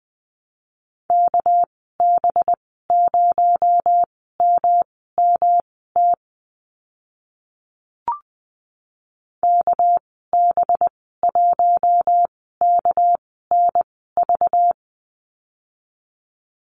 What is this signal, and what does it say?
Morse code, heard as a keyed tone, about 20 wpm, spaced as for 14 wpm, 700 Hz. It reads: KB0MMT KB1KNV
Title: 231 Lesson 37g - Letter x - Call Signs 20wpm